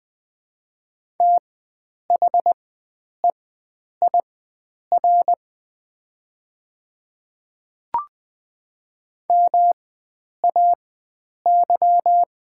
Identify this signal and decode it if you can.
Morse code, heard as a keyed tone, about 20 wpm, spaced as for 9 wpm, 690 Hz. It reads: THEIR MAY
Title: 4x Character Spacing - Top 100 Words - 20wpm